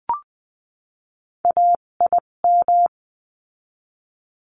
Morse code, heard as a keyed tone, about 20 wpm, spaced as for 17 wpm, 700 Hz. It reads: AIM
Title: 257 Lesson 42b - Words 14-20wpm